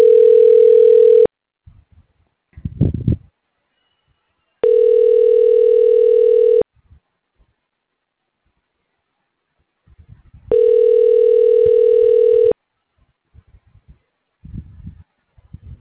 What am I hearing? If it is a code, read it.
Ringback tone (North America), 440 + 480 Hz.